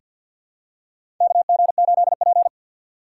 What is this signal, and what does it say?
Morse code, heard as a keyed tone, about 51 wpm, 700 Hz. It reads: KG8P
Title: USA Call Signs - 6x Character Spacing - 50wpm